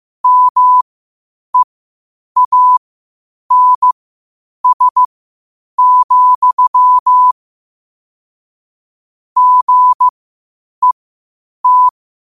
Morse code, heard as a keyed tone, about 15 wpm, spaced as for 8 wpm, 1000 Hz. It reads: MEANS, GET
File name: qotdfiles / 8wpm